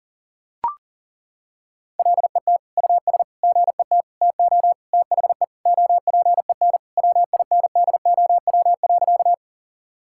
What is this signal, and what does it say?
Morse code, heard as a keyed tone, about 40 wpm, 700 Hz. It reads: LET US GET TO THE OPEN WINDOW.
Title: Sentences from Top 1600 Words 40wpm